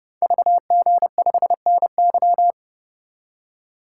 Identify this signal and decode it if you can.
Morse code, heard as a keyed tone, about 30 wpm, 700 Hz. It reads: VG5NY
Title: Code Groups - 5 Letter:Numbers - 30wpm